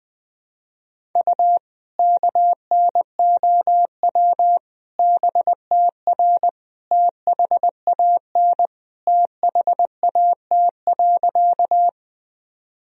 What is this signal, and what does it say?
Morse code, heard as a keyed tone, about 21 wpm, 700 Hz. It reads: U KNOW BTR THAN THAT.
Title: Sentences from Top 400 Words and Abbreviations 20wpm